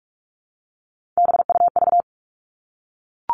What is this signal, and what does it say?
Morse code, heard as a keyed tone, about 44 wpm, 700 Hz. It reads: BUV
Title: All Cut Numbers - 3-Digit 45wpm